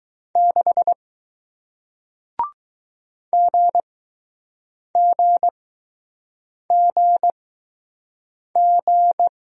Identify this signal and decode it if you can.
Morse code, heard as a keyed tone, about 20 wpm, spaced as for 6 wpm, 700 Hz. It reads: 6 GGGG